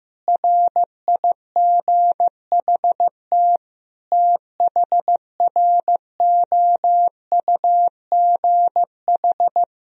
Morse code, heard as a keyed tone, about 16 wpm, 700 Hz. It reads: RIGHT THROUGH
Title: Sentences from Top 200 Words and Abbreviations 15wpm